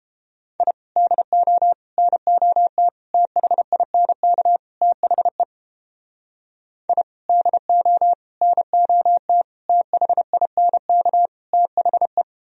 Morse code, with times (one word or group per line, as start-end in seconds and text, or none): 0.6-0.7 I
1.0-1.7 DO
2.0-2.9 NOT
3.1-4.6 THINK
4.8-5.4 THE
6.9-7.0 I
7.3-8.1 DO
8.4-9.4 NOT
9.7-11.3 THINK
11.5-12.2 THE